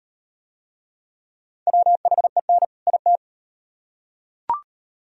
Morse code, heard as a keyed tone, about 34 wpm, 700 Hz. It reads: WHEN IT